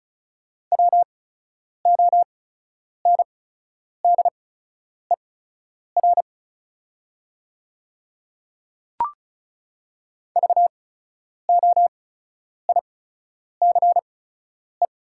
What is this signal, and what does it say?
Morse code, heard as a keyed tone, about 36 wpm, spaced as for 10 wpm, 700 Hz. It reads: WONDER VOICE
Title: 8x Character Spacing - Top 500 Words - 35wpm